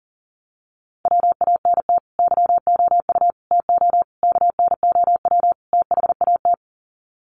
Morse code, heard as a keyed tone, about 40 wpm, 700 Hz. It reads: WANT YOU TO KNOW THAT
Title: Sets of 5 Words Encore 40wpm